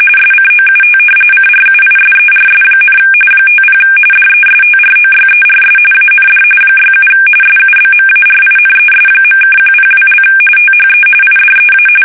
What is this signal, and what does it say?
RTTY, about 45.45 baud. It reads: SCOUT CRUISERS, 50 DESTROYERS AND 67 SUBMARINES.(102) THE IDEA
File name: fsk850